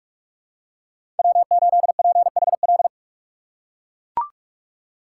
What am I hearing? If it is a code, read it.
Morse code, heard as a keyed tone, about 44 wpm, 700 Hz. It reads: W8PHL
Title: USA Call Signs - 5 Characters - 2x Character Spacing - 45wpm